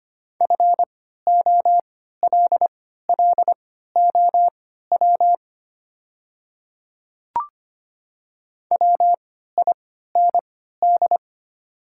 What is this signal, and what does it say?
Morse code, heard as a keyed tone, about 25 wpm, spaced as for 14 wpm, 700 Hz. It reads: FOLLOW WIND
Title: 3x Character Spacing - Top 500 Words - 25wpm